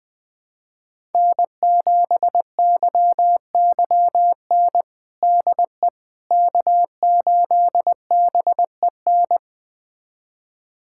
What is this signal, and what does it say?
Morse code, heard as a keyed tone, about 20 wpm, 700 Hz. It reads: N7YYN DE K8BEN
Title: USA Callsign DE Callsign - 5 Characters - 20wpm